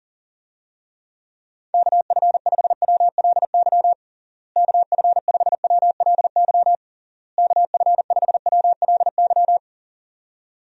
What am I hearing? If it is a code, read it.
Morse code, heard as a keyed tone, about 40 wpm, 700 Hz. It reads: KF5WLY KF5WLY KF5WLY
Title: USA Call Signs - 6 Characters - Repeated 3x - 40wpm